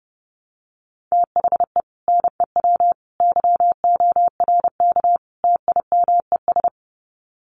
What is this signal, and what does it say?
Morse code, heard as a keyed tone, about 30 wpm, 700 Hz. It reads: THE NEW YORK TIMES
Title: Sets of 4 Words 30wpm